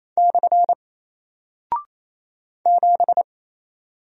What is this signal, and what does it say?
Morse code, heard as a keyed tone, about 28 wpm, 700 Hz. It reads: / 7
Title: Single Character (Letter:Number:Punctuation:Prosign) Alt 28wpm